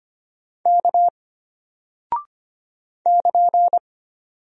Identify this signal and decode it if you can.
Morse code, heard as a keyed tone, about 25 wpm, 700 Hz. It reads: K <KN>
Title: Single Character (Letter:Number:Punctuation:Prosign) Alt 25wpm